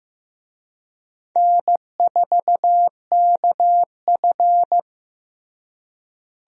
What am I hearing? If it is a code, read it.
Morse code, heard as a keyed tone, about 15 wpm, 700 Hz. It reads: N4KF
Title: USA Call Signs - 4 Characters - Repeated 2x - 15wpm